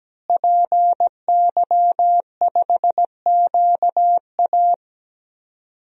Morse code, heard as a keyed tone, about 17 wpm, 700 Hz. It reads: PY5QA
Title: DXCC-100 Call Signs - 5 Characters - Repeated 3x - 17wpm